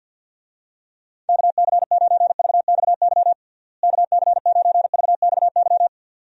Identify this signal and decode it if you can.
Morse code, heard as a keyed tone, about 53 wpm, 700 Hz. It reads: KC9VXY KC9VXY
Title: USA Call Signs - 6 Characters - Repeated 2x - 50wpm